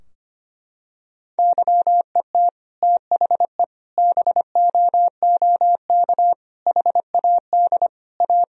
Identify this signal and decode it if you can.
Morse code, heard as a keyed tone, about 25 wpm, 700 Hz. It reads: YET THE BOOK HAD A
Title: Sentences from Top 1000 Words - Review Part 4 - 25wpm